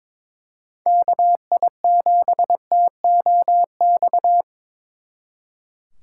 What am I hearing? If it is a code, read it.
Morse code, heard as a keyed tone, about 25 wpm, 700 Hz. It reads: KI7TOX
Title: Speed Racing - USA 2x3 Call Signs 22wpm